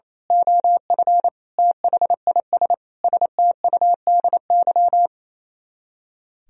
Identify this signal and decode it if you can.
Morse code, heard as a keyed tone, about 26 wpm, 700 Hz. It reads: OF THIS STUDY
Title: Speed-Racing - Sets of 5 Words 28wpm